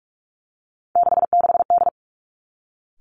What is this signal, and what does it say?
Morse code, heard as a keyed tone, about 46 wpm, 700 Hz. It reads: 66D
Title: Three Challenging Letter-Number Combinations 45wpm